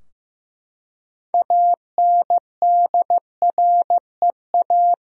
Morse code, heard as a keyed tone, about 15 wpm, 700 Hz. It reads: ANDREA
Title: USA Common Names - 15wpm